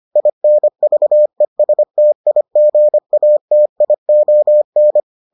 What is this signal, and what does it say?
Morse code, heard as a keyed tone, about 25 wpm, 600 Hz. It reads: INVESTIGATION